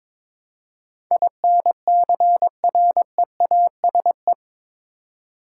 Morse code, heard as a keyed tone, about 21 wpm, 700 Hz. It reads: INCREASE
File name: practice / 22wpm / Eight Letter Words - 22wpm